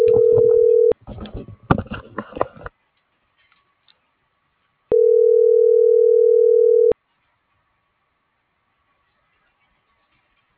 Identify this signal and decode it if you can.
Ringback tone (North America), 440 + 480 Hz.
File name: CENTRAL SOTOVA / out alejandro.rodriguez - .558